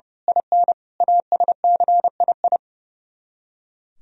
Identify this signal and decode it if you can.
Morse code, heard as a keyed tone, about 29 wpm, 700 Hz. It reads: IN ASCII
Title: Speed-Racing - Sets of 5 Words 30wpm